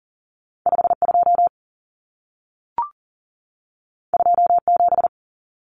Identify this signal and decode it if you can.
Morse code, heard as a keyed tone, about 40 wpm, 700 Hz. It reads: 52 27
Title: Numbers - 2-Digit 40wpm